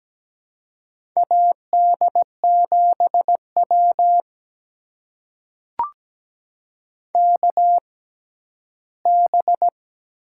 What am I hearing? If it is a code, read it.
Morse code, heard as a keyed tone, about 17 wpm, spaced as for 11 wpm, 700 Hz. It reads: AD7W KB
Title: USA Call Signs - 6x Character Spacing - 17wpm